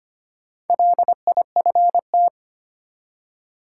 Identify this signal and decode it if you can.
Morse code, heard as a keyed tone, about 25 wpm, 700 Hz. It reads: LIFT
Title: Four Letter Words - 25wpm